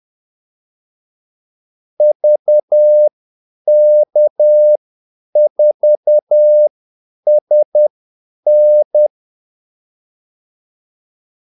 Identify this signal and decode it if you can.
Morse code, heard as a keyed tone, about 10 wpm, spaced as for 8 wpm, 600 Hz. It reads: VK4SN
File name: Callsigns-10-8wpm